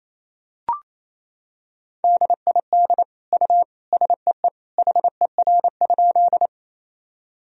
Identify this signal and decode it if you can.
Morse code, heard as a keyed tone, about 28 wpm, 700 Hz. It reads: DID U SEE HER?
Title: Sentences from Top 100 Words and Abbreviations 28wpm